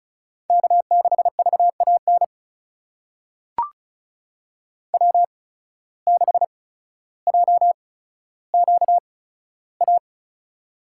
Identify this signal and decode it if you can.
Morse code, heard as a keyed tone, about 34 wpm, spaced as for 15 wpm, 700 Hz. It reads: K6VAN W6JQA
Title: USA Call Signs - 5 Characters - 8x Character Spacing - 35wpm